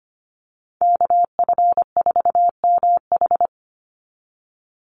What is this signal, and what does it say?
Morse code, heard as a keyed tone, about 25 wpm, 700 Hz. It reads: KF4MH
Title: NA SOTA Chaser Callsigns 25wpm